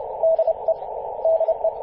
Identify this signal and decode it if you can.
Morse code, heard as a keyed tone, about 30 wpm, 650 Hz, with radio noise. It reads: DE DE
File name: 5w0xg-18-075mhz-cw